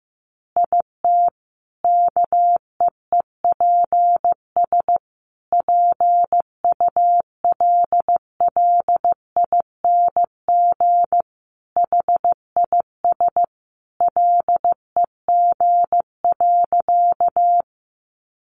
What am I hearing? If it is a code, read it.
Morse code, heard as a keyed tone, about 15 wpm, 700 Hz. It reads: IT KEEPS PULLING HIS LEG.
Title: Additional Easy Phrases 15wpm